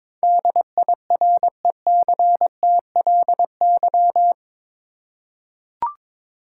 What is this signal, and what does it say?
Morse code, heard as a keyed tone, about 21 wpm, 700 Hz. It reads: DIRECTLY
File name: practice / 22wpm / Eight Letter Words - 22wpm